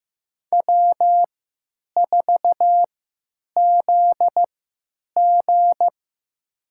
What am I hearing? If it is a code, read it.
Morse code, heard as a keyed tone, about 15 wpm, spaced as for 9 wpm, 700 Hz. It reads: W4ZG
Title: USA Call Signs - 4 Characters - 3x Character Spacing - 15wpm